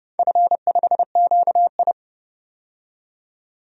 Morse code, heard as a keyed tone, about 30 wpm, 700 Hz. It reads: F5QI
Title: DXCC-100 Call Signs - 4 Characters - 30wpm